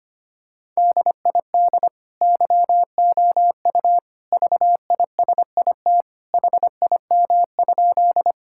Morse code, heard as a keyed tone, about 25 wpm, 700 Hz. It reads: DID YOU VISIT HIM?
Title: Sentences from Top 1000 Words - Review Part 6 - 25wpm